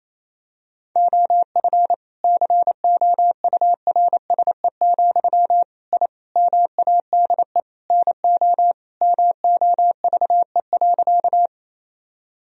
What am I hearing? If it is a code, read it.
Morse code, heard as a keyed tone, about 28 wpm, 700 Hz. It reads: OF COURSE, I MADE NO MOVE.
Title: Sentences from Top 1000 Words - Review Part 1 - 28wpm